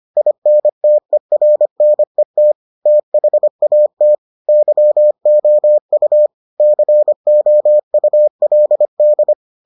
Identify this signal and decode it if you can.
Morse code, heard as a keyed tone, about 25 wpm, 600 Hz. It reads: INTERNET THAT YOU COULD